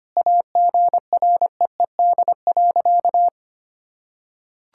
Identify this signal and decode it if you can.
Morse code, heard as a keyed tone, about 25 wpm, 700 Hz. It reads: AGREED.
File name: practice / 25wpm / Sentences with 9-Word Phrases - 25wpm